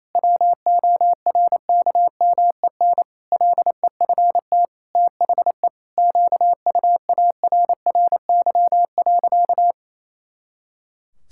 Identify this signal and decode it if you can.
Morse code, heard as a keyed tone, about 28 wpm, 700 Hz. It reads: WORKMEN LEFT THE QUARRY.
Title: Pangram Challenge v2 - 28wpm